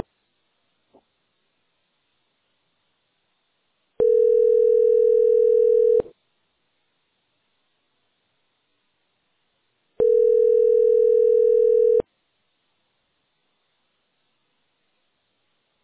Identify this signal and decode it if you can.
Ringback tone (North America), 440 + 480 Hz.